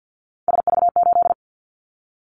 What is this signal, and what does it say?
Morse code, heard as a keyed tone, about 50 wpm, 700 Hz. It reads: SV8
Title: Callsign Prefixes for Top 50 DXCC - 50wpm